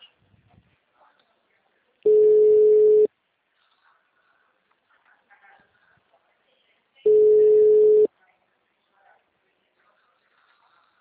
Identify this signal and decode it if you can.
Ringback tone (Europe, 425 Hz).
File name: out- -2011- - - .33601